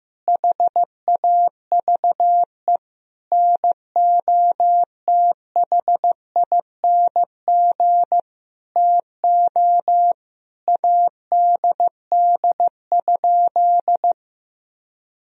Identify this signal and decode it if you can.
Morse code, heard as a keyed tone, about 15 wpm, 700 Hz. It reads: HAVE NOTHING TO ADD?
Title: Sentences from Top 1000 Words - Review Part 1 - 15wpm